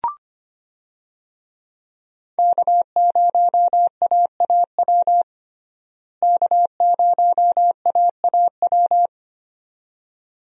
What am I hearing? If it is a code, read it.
Morse code, heard as a keyed tone, about 25 wpm, 700 Hz. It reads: K0AAW K0AAW
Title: USA Call Signs - 5 Characters - Repeated 2x - 25wpm